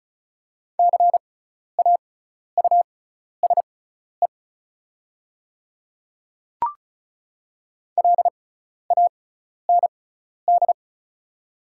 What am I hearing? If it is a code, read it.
Morse code, heard as a keyed tone, about 36 wpm, spaced as for 12 wpm, 690 Hz. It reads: CAUSE LAND
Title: 6x Character Spacing - Top 200 Words - 35wpm